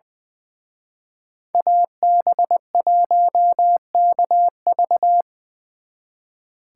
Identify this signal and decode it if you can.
Morse code, heard as a keyed tone, about 23 wpm, 700 Hz. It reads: AB1KV
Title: Speed Racing - USA Call Signs 20wpm